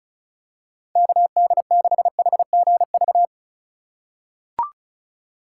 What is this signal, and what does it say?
Morse code, heard as a keyed tone, about 36 wpm, 700 Hz. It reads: KD6HGV
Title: USA Call Signs - 6 Characters - 3x Character Spacing - 35wpm